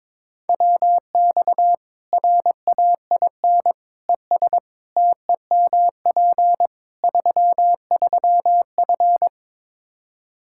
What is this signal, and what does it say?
Morse code, heard as a keyed tone, about 22 wpm, 700 Hz. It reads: WX RAIN ES TEMP 33F
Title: Practice QSO - Exchange 2 - Info - 22wpm